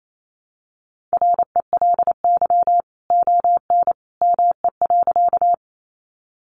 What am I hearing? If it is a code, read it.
Morse code, heard as a keyed tone, about 28 wpm, 700 Hz. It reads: RELY ON ME.
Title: Sentences from Top 1200 Words 28wpm